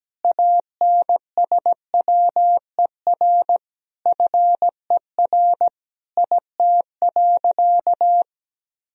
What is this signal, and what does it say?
Morse code, heard as a keyed tone, about 17 wpm, 700 Hz. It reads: ANSWER FER IT.
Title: Sentences from Top 300 Words and Abbreviations 17wpm